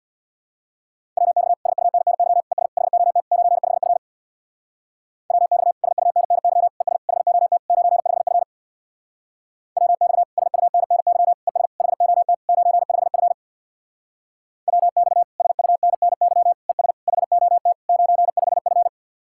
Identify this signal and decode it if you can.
Morse code, heard as a keyed tone, about 65 wpm, 700 Hz. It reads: WX SUNNY ES HOT 95F WX SUNNY ES HOT 95F WX SUNNY ES HOT 95F WX SUNNY ES HOT 95F